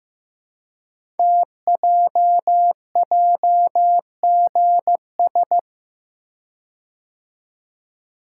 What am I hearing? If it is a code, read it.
Morse code, heard as a keyed tone, about 15 wpm, 700 Hz. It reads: TJJGS
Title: Code Groups - 5 Letters - 15wpm